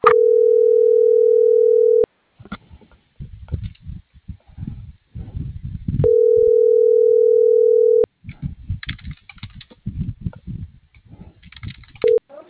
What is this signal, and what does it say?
Ringback tone (North America), 440 + 480 Hz.